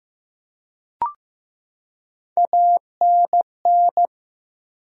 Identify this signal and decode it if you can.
Morse code, heard as a keyed tone, about 15 wpm, 700 Hz. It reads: ANN
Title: USA Common Names - Max 3 Letters - 15wpm